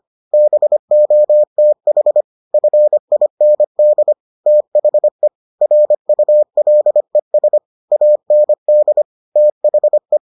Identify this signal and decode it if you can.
Morse code, heard as a keyed tone, about 25 wpm, 600 Hz. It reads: BOTH FIND THE RULES AND THE